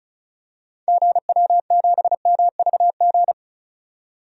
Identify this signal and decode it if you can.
Morse code, heard as a keyed tone, about 36 wpm, 690 Hz. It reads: GW7MVG
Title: DXCC-100 Call Signs - 6 Characters - 35wpm